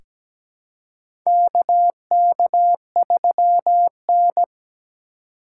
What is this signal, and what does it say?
Morse code, heard as a keyed tone, about 17 wpm, 700 Hz. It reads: KK3N
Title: USA Call Signs - 4 Characters - 4x Character Spacing - 17wpm